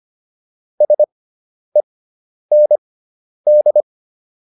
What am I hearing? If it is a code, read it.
Morse code, heard as a keyed tone, about 25 wpm, spaced as for 10 wpm, 600 Hz. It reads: SEND